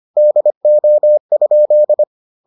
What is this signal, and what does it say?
Morse code, heard as a keyed tone, about 25 wpm, 600 Hz. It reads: DO?